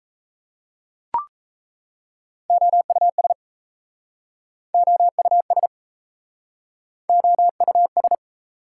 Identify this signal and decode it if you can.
Morse code, heard as a keyed tone, about 38 wpm, 700 Hz. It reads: OUS OUS OUS